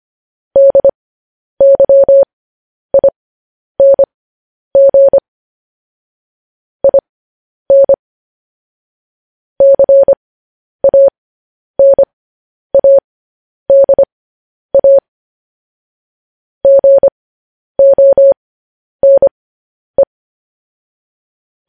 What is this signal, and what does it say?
Morse code, heard as a keyed tone, about 25 wpm, spaced as for 10 wpm, 550 Hz. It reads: DYING IN CANADA GONE
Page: news headlines in morse code at 10 wpm